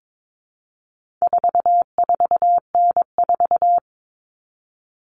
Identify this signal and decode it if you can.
Morse code, heard as a keyed tone, about 22 wpm, 700 Hz. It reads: 44N4
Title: All Cut Numbers - 4-Digit 22wpm